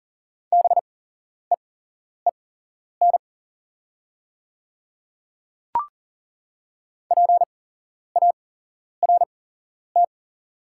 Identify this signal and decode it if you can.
Morse code, heard as a keyed tone, about 42 wpm, spaced as for 11 wpm, 700 Hz. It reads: BEEN PART
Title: 8x Character Spacing - Top 200 Words - 40wpm